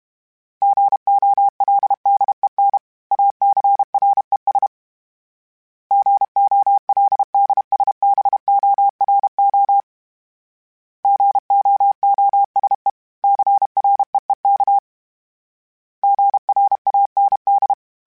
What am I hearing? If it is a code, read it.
Morse code, heard as a keyed tone, about 32 wpm, 800 Hz. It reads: GOLDEN ACRES GOLDSBORO GOOSE CREEK GRAND
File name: fromSMs / FromSMs mp3s / Cities@30